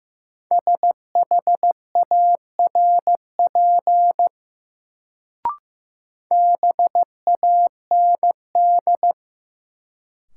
Morse code, heard as a keyed tone, about 15 wpm, 700 Hz. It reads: SHARP BAND
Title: Sentences from Top 1600 Words - New Word Drill - 15wpm